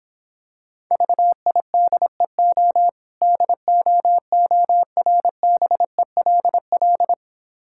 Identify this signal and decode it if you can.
Morse code, heard as a keyed tone, about 26 wpm, 700 Hz. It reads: VIDEO DOORBELL